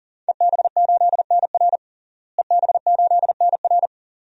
Morse code, heard as a keyed tone, about 40 wpm, 700 Hz. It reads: EB8NR EB8NR
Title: DXCC-100 Call Signs - 5 Characters - Repeated 2x - 40wpm